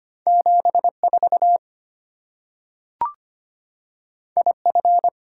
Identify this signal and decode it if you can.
Morse code, heard as a keyed tone, about 25 wpm, 700 Hz. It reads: 74 IF
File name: Two Letter-Number Combinations - 25wpm